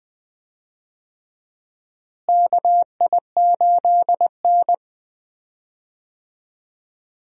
Morse code, heard as a keyed tone, about 20 wpm, 700 Hz. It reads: KI8N
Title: USA Call Signs - 4 Characters - Repeated 2x - 20wpm